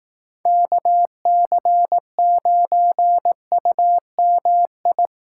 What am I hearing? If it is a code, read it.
Morse code, heard as a keyed tone, about 19 wpm, 700 Hz. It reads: KC9UMI
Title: Speed Racing - USA 2x3 Call Signs 15wpm